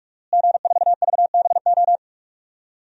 Morse code, heard as a keyed tone, about 41 wpm, 700 Hz. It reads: G4VBO
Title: DXCC-100 Call Signs - 5 Characters - Repeated 3x - 45wpm